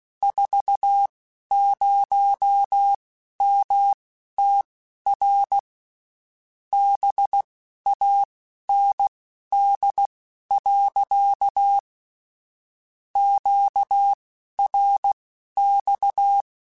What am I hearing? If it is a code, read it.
Morse code, heard as a keyed tone, about 16 wpm, spaced as for 12 wpm, 800 Hz. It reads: 40MTR BAND. QRX
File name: (spaced) / QSOs Spaced-mp3s / kc7sac @15 sp